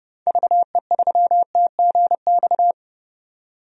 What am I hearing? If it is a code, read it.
Morse code, heard as a keyed tone, about 32 wpm, 700 Hz. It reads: VE3TGX
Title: DXCC-100 Call Signs - 6 Characters - 30wpm